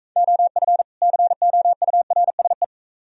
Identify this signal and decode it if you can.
Morse code, heard as a keyed tone, about 43 wpm, 700 Hz. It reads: OF COURSE